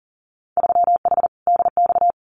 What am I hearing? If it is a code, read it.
Morse code, heard as a keyed tone, about 38 wpm, 700 Hz. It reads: 3H DX
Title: Winter Field Day Exchanges Encore 40wpm